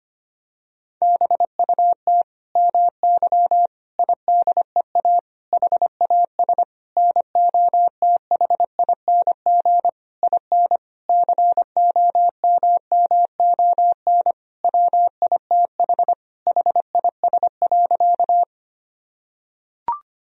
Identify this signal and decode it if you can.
Morse code, heard as a keyed tone, about 25 wpm, 700 Hz. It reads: BUT MY IDEA HAS NOTHING IN COMMON WITH HIS.
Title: Sentences from Top 1000 Words - Review Part 3 - 25wpm